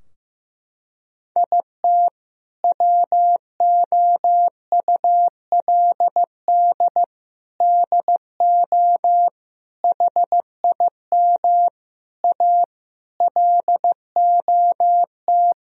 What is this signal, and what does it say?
Morse code, heard as a keyed tone, about 15 wpm, 700 Hz. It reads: IT WOULD DO HIM A LOT
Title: Sentences from Top 1000 Words - Review Part 3 - 15wpm